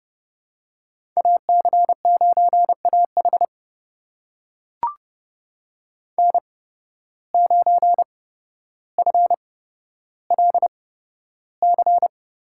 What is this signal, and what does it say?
Morse code, heard as a keyed tone, about 29 wpm, spaced as for 13 wpm, 700 Hz. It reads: AC9AH N9FLC
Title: USA Call Signs - 5 Characters - 8x Character Spacing - 30wpm